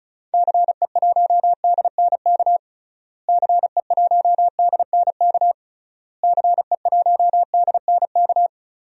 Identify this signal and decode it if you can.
Morse code, heard as a keyed tone, about 36 wpm, 700 Hz. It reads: CE1DNK CE1DNK CE1DNK
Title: DXCC-100 Call Signs - 6 Characters - Repeated 3x - 35wpm